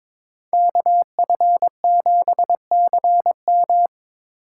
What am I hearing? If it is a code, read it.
Morse code, heard as a keyed tone, about 21 wpm, 700 Hz. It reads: KF7CM
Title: USA Call Signs - 8x Character Spacing - 22wpm